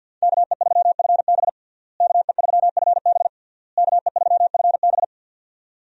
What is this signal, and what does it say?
Morse code, heard as a keyed tone, about 50 wpm, 700 Hz. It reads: KE3FB KE3FB KE3FB
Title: USA Call Signs - 5 Characters - Repeated 3x - 50wpm